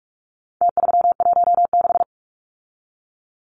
Morse code, heard as a keyed tone, about 44 wpm, 700 Hz. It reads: T316
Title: Numbers - Contest Style - 4-Digit 45wpm